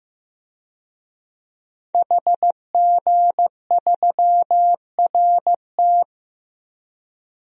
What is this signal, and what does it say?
Morse code, heard as a keyed tone, about 15 wpm, 700 Hz. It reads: HG3RT
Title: DXCC-100 Call Signs - 5 Characters - Repeated 3x - 15wpm